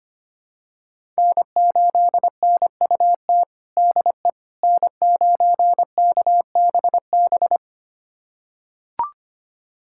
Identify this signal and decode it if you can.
Morse code, heard as a keyed tone, about 25 wpm, 700 Hz. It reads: N8NUT DE N9KBB
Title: USA Callsign DE Callsign - 5 Characters - 25wpm